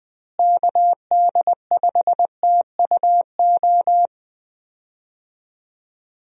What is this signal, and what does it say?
Morse code, heard as a keyed tone, about 20 wpm, 700 Hz. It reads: KD5TUO
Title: USA Call Signs - 6 Characters - Repeated 2x - 20wpm